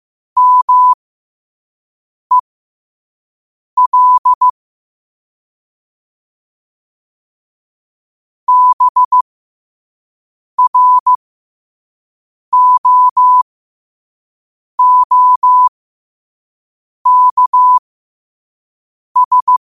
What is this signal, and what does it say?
Morse code, heard as a keyed tone, about 15 wpm, spaced as for 5 wpm, 1000 Hz. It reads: MEL BROOKS
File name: qotdfiles / 5wpm